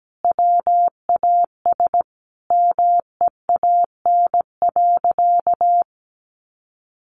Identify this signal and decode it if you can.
Morse code, heard as a keyed tone, about 17 wpm, 700 Hz. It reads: WAS MEAN.
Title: Sentences from Top 1500 Words 17wpm